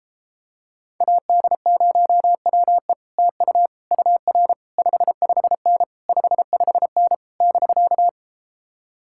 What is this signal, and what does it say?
Morse code, heard as a keyed tone, about 33 wpm, 700 Hz. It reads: AD0WE TU UR 55N 55N <BK>